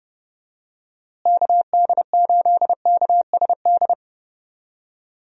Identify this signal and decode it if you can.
Morse code, heard as a keyed tone, about 30 wpm, 700 Hz. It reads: KD8KSD
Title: USA Call Signs - 6 Characters - Repeated 2x - 30wpm